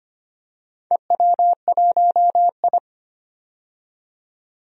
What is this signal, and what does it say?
Morse code, heard as a keyed tone, about 25 wpm, 700 Hz. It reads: EW1I
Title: DXCC-100 Call Signs - 4 Characters - 25wpm